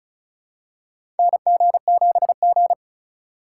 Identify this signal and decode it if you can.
Morse code, heard as a keyed tone, about 34 wpm, 700 Hz. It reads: NG7G
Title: USA Call Signs - 4x Character Spacing - 35wpm